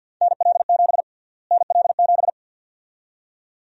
Morse code, heard as a keyed tone, about 51 wpm, 700 Hz. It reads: NL7 NL7
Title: USA Call Sign Prefixes - Repeated 2x - 50wpm